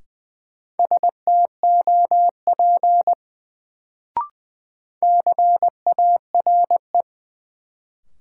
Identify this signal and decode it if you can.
Morse code, heard as a keyed tone, about 20 wpm, 700 Hz. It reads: STOP CARE
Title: Sentences from Top 300 Words - New Word Drill - 20wpm